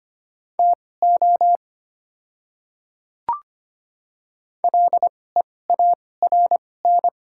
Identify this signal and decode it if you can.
Morse code, heard as a keyed tone, about 25 wpm, spaced as for 18 wpm, 690 Hz. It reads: TO LEARN
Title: 2x Character Spacing - Top 300 Words - 25wpm